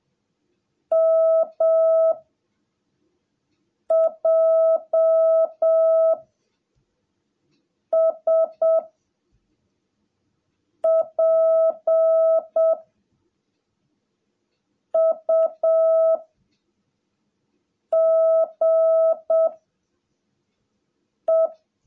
Morse code, heard as a keyed tone, about 7 wpm, 660 Hz. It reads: M J S P U G E
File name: jh8cbh / 2021cw / cw17a